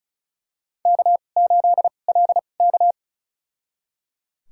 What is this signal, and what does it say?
Morse code, heard as a keyed tone, about 34 wpm, spaced as for 25 wpm, 700 Hz. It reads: K8LK
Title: USA Call Signs - 2x Character Spacing - 35wpm